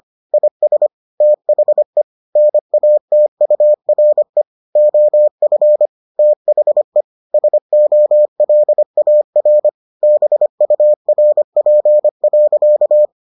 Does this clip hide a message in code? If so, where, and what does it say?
Yes: Morse code from 0.3 to 13.1 s, heard as a keyed tone, about 25 wpm, 600 Hz. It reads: IS THE NATURE OF THE SOLAR BURP.